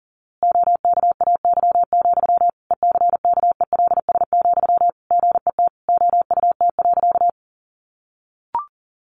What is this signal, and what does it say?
Morse code, heard as a keyed tone, about 40 wpm, 700 Hz. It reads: OKAY, ECKELS, GET OUT.
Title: A Sound Of Thunder 40wpm